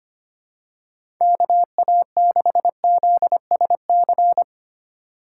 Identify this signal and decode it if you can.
Morse code, heard as a keyed tone, about 25 wpm, 700 Hz. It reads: KA6ZSC
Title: USA Call Signs - 6 Characters - 4x Character Spacing - 25wpm